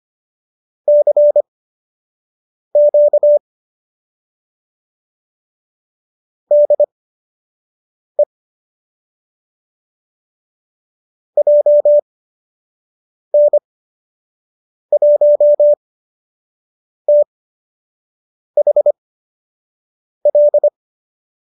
Morse code, heard as a keyed tone, about 25 wpm, spaced as for 6 wpm, 590 Hz. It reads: CQ DE JN1THL
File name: qso003 06